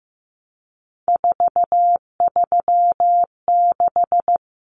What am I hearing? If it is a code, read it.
Morse code, heard as a keyed tone, about 15 wpm, 700 Hz. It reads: 436
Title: Numbers - 3-Digit 15wpm